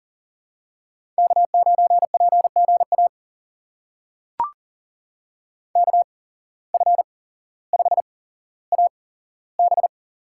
Morse code, heard as a keyed tone, about 40 wpm, spaced as for 17 wpm, 700 Hz. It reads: K9PGA KF5AB
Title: USA Call Signs - 5 Characters - 8x Character Spacing - 40wpm